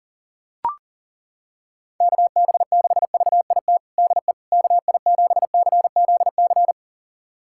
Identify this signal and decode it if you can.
Morse code, heard as a keyed tone, about 40 wpm, 700 Hz. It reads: KB6VIT DE KI7CZC
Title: USA Callsign DE Callsign - 6 Characters - 40wpm